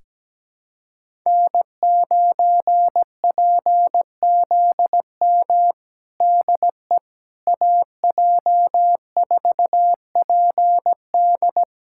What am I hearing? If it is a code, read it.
Morse code, heard as a keyed tone, about 17 wpm, 700 Hz. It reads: N9PZM DE AJ4PD
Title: USA Callsign DE Callsign - 5 Characters - 17wpm